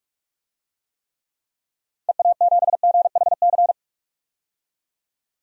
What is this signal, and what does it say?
Morse code, heard as a keyed tone, about 46 wpm, 700 Hz. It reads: EA7GHC
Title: DXCC-100 Call Signs - 6 Characters - 45wpm